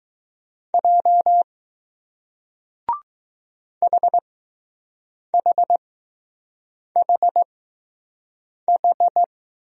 Morse code, heard as a keyed tone, about 20 wpm, spaced as for 6 wpm, 700 Hz. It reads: J HHHH